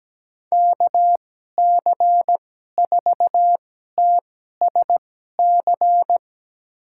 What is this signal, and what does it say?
Morse code, heard as a keyed tone, about 17 wpm, spaced as for 12 wpm, 700 Hz. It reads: KC4TSC
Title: USA Call Signs - 6 Characters - 2x Character Spacing - 17wpm